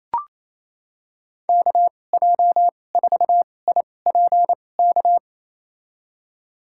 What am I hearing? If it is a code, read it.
Morse code, heard as a keyed tone, about 28 wpm, spaced as for 20 wpm, 700 Hz. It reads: KJ4IPK
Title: USA Call Signs - 6 Characters - 2x Character Spacing - 28wpm